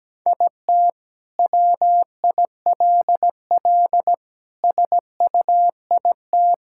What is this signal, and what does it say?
Morse code, heard as a keyed tone, about 17 wpm, 700 Hz. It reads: IT WILL SUIT
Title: Sentences from Top 1600 Words 17wpm